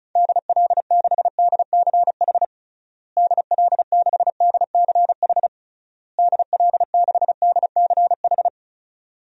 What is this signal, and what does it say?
Morse code, heard as a keyed tone, about 34 wpm, 700 Hz. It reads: DL6DCH DL6DCH DL6DCH
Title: DXCC-100 Call Signs - 6 Characters - Repeated 3x - 35wpm